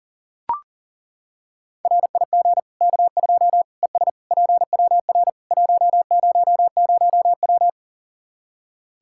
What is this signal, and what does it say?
Morse code, heard as a keyed tone, about 40 wpm, 700 Hz. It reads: RIG K2 ES PWR 100W
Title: Practice QSO - Exchange 2 - Info - 40wpm